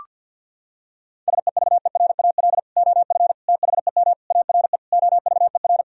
Morse code, heard as a keyed tone, about 50 wpm, 700 Hz. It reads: SEVERAL OF THEM ARE OVER
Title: Sentences from Top 1000 Words - Review Part 2 - 50wpm